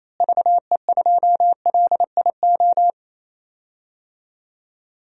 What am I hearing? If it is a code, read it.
Morse code, heard as a keyed tone, about 28 wpm, 700 Hz. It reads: VE2LIO
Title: DXCC-100 Call Signs - 6 Characters - Repeated 2x - 28wpm